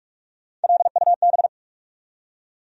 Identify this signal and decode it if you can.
Morse code, heard as a keyed tone, about 46 wpm, 700 Hz. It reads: LUB
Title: USA Call Sign Postfixes - 3 Letters 45wpm